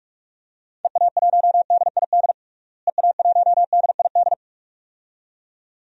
Morse code, heard as a keyed tone, about 46 wpm, 700 Hz. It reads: EA1DID EA1DID
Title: DXCC-100 Call Signs - 6 Characters - Repeated 2x - 45wpm